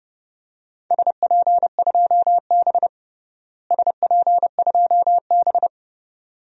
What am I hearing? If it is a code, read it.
Morse code, heard as a keyed tone, about 32 wpm, 700 Hz. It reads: SP2B SP2B
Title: DXCC-100 Call Signs - 4 Characters - Repeated 2x - 30wpm